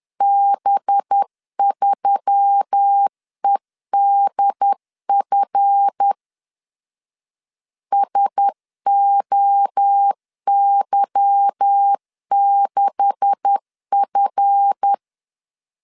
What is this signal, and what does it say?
Morse code, heard as a keyed tone, about 10 wpm, 800 Hz. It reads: B3EDF SOY6F
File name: LEZIONI / 9 286 10wpm 2735